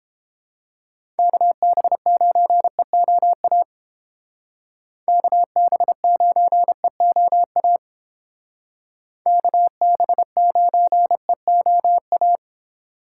Morse code, heard as a keyed tone, about 29 wpm, 700 Hz. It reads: KB9EOA KB9EOA KB9EOA